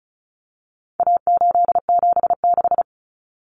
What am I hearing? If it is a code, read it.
Morse code, heard as a keyed tone, about 34 wpm, 700 Hz. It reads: A876
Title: Common Cut Numbers - 4-Digit 35wpm